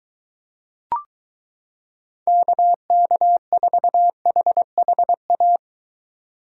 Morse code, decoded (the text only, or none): KK4HHA